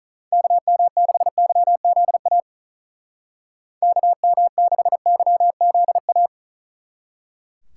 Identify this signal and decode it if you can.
Morse code, heard as a keyed tone, about 43 wpm, 700 Hz. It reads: KM6YZA KM6YZA
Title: Speed Racing - USA Call Signs 35wpm